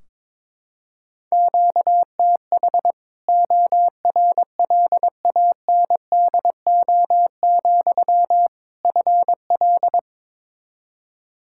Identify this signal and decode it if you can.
Morse code, heard as a keyed tone, about 22 wpm, 700 Hz. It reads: QTH ORLANDO, FL
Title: Practice QSO - Exchange 1 - Beginning - 22wpm